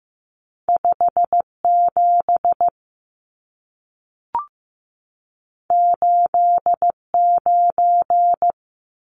Morse code, heard as a keyed tone, about 15 wpm, 700 Hz. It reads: 57 89
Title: Numbers - 2-Digit 15wpm